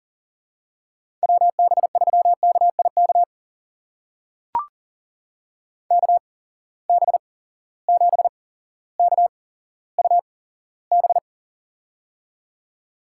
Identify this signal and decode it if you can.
Morse code, heard as a keyed tone, about 38 wpm, spaced as for 17 wpm, 700 Hz. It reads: WB3KIK KB7KUB
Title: USA Call Signs - 6 Characters - 8x Character Spacing - 40wpm